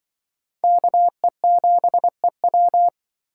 Morse code, heard as a keyed tone, about 24 wpm, 700 Hz. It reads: KE7EW
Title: Speed Racing - USA 2x2 Call Signs 20wpm